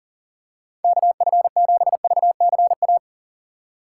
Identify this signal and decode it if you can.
Morse code, heard as a keyed tone, about 37 wpm, 700 Hz. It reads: KF7VCA
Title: USA Call Signs - 6 Characters - 8x Character Spacing - 40wpm